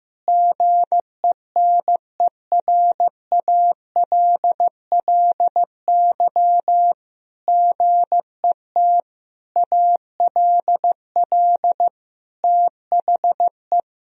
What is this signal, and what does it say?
Morse code, heard as a keyed tone, about 15 wpm, 700 Hz. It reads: GENERALLY GET ALL THE
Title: Sentences from Top 1000 Words - Review Part 4 - 15wpm